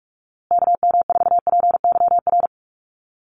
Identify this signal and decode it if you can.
Morse code, heard as a keyed tone, about 44 wpm, 700 Hz. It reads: KM4PYR
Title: USA Call Signs Repeated 2x 45wpm